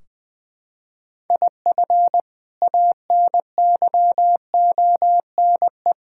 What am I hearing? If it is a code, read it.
Morse code, heard as a keyed tone, about 20 wpm, 700 Hz. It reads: IF ANYONE
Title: Sentences from Top 1600 Words 20wpm